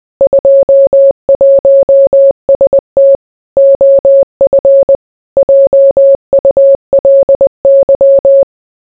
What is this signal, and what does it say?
Morse code, heard as a keyed tone, about 20 wpm, 550 Hz. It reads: 21ST OF JULY